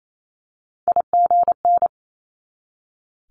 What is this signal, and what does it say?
Morse code, heard as a keyed tone, about 28 wpm, 700 Hz. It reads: IGN
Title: Multi-letter Phonemes 28wpm